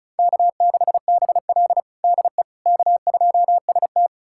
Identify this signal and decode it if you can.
Morse code, heard as a keyed tone, about 36 wpm, 700 Hz. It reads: K6BL DE K2ST
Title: USA Callsign DE Callsign - 4 Characters - 35wpm